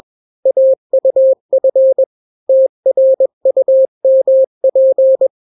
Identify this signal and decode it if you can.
Morse code, heard as a keyed tone, about 21 wpm, 530 Hz. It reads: AUF TRUMP
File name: daily / 20wpm